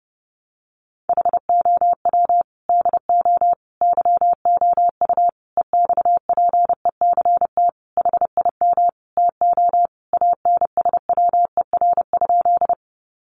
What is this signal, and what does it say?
Morse code, heard as a keyed tone, about 30 wpm, 700 Hz. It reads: HOW DO YOU EXPECT HIM TO ANSWER?
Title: Sentences from Top 1100 Words 30wpm